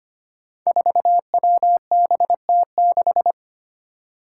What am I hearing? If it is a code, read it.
Morse code, heard as a keyed tone, about 25 wpm, 700 Hz. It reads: 4WBT6
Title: Five Challenging Letter-Number Combinations 25wpm